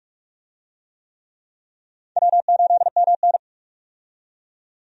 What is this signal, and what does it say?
Morse code, heard as a keyed tone, about 46 wpm, 700 Hz. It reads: W8MN
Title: USA Call Signs - 4 Characters - 45wpm